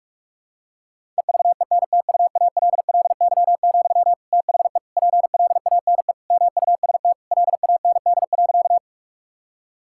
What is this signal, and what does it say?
Morse code, heard as a keyed tone, about 41 wpm, 700 Hz. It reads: EVENTUALLY, THE PLANE MUST LAND.
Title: Sentences from Top 1000 Words - Review Part 6 - 45wpm